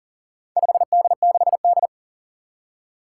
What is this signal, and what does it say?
Morse code, heard as a keyed tone, about 43 wpm, 700 Hz. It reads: 5D6D
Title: Four Challenging Letter-Number Combinations 40wpm